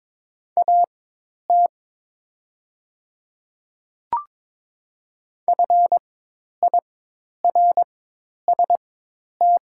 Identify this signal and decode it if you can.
Morse code, heard as a keyed tone, about 22 wpm, spaced as for 10 wpm, 700 Hz. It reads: AT FIRST
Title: 4x Character Spacing - Top 100 Words - 22wpm